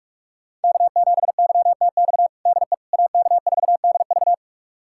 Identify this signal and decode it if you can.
Morse code, heard as a keyed tone, about 46 wpm, 700 Hz. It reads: K7YTX DE AK4DV
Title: USA Callsign DE Callsign - 5 Characters - 45wpm